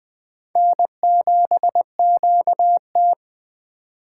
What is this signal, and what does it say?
Morse code, heard as a keyed tone, about 20 wpm, 700 Hz. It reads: N7QT
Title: K1USN SST QSO Practice - 20wpm